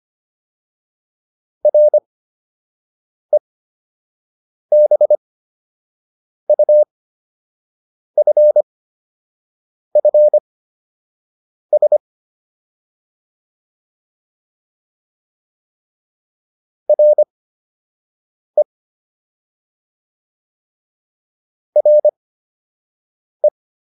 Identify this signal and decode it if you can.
Morse code, heard as a keyed tone, about 25 wpm, spaced as for 6 wpm, 600 Hz. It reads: REBUFFS RE RE